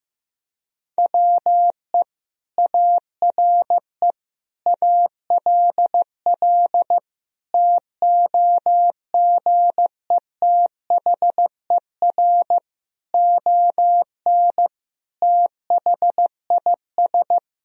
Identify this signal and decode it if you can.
Morse code, heard as a keyed tone, about 16 wpm, 700 Hz. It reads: WE ARE ALL TOGETHER ON THIS
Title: Sentences from Top 1000 Words - Review Part 2 - 15wpm